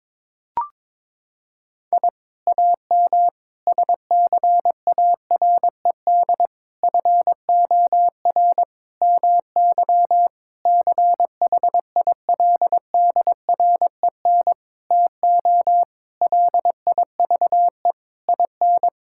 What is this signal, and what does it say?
Morse code, heard as a keyed tone, about 22 wpm, 700 Hz. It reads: I AM SCARED FOR MY CHILDREN TO LIVE IN
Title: Sentences from Top 1000 Words - Review Part 5 - 22wpm